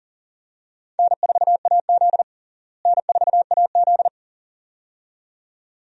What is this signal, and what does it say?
Morse code, heard as a keyed tone, about 40 wpm, 700 Hz. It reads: N4AZ N4AZ
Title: USA Call Signs - 4 Characters - Repeated 2x - 40wpm